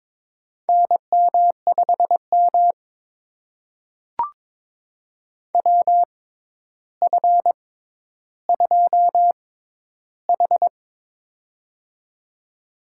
Morse code, heard as a keyed tone, about 22 wpm, spaced as for 11 wpm, 700 Hz. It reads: NM5M WF2H
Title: USA Call Signs - 4 Characters - 6x Character Spacing - 22wpm